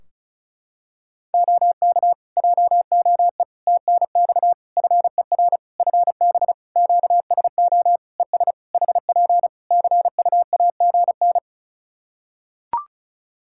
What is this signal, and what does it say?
Morse code, heard as a keyed tone, about 34 wpm, 700 Hz. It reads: OK JOE TNX FER FB QSO ES HP CUAGN
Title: K1USN SST QSO Practice 35wpm